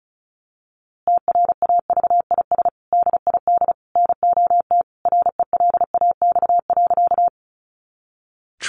Morse code, heard as a keyed tone, about 34 wpm, 700 Hz. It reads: TRAVIS DID NOT RELAX.
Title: A Sound Of Thunder 35wpm